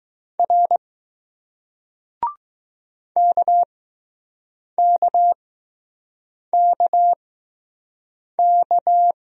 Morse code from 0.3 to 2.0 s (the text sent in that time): R